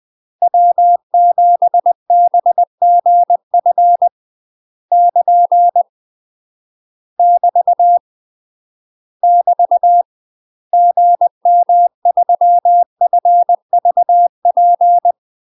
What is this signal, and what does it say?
Morse code, heard as a keyed tone, about 20 wpm, 700 Hz. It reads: W7BGF <KN> <BT> <BT> GM3FVP
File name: UK MASTER QSOs TRAINING / 20WPM / UK MASTER QSOs 0023